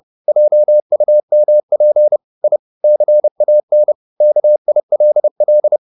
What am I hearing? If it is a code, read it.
Morse code, heard as a keyed tone, about 30 wpm, 600 Hz. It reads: JUMP I CAN KILL
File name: the old man and the sea / the old man and the sea 1 30wpm